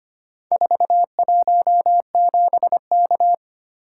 Morse code, heard as a keyed tone, about 25 wpm, 700 Hz. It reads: 417K
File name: practice / 25wpm / Four Letter-Number Combinations - 25wpm